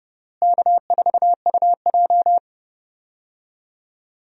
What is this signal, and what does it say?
Morse code, heard as a keyed tone, about 31 wpm, 700 Hz. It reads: K4UJ
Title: USA Call Signs - 4 Characters - Repeated 3x - 30wpm